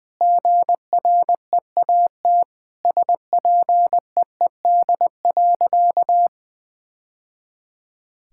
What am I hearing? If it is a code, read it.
Morse code, heard as a keyed tone, about 21 wpm, 700 Hz. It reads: GREAT SPEED.
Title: Sentences from Top 1000 Words - Review Part 5 - 20wpm